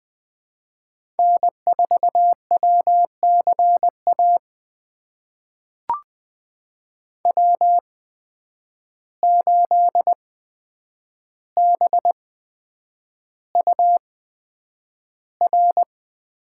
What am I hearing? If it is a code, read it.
Morse code, heard as a keyed tone, about 20 wpm, spaced as for 9 wpm, 700 Hz. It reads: N4WCA W8BUR
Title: USA Call Signs - 5 Characters - 8x Character Spacing - 20wpm